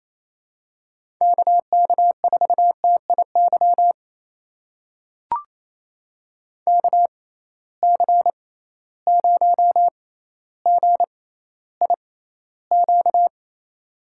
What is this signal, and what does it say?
Morse code, heard as a keyed tone, about 28 wpm, spaced as for 14 wpm, 700 Hz. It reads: KK4TIY KC0GIQ
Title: USA Call Signs - 6 Characters - 6x Character Spacing - 28wpm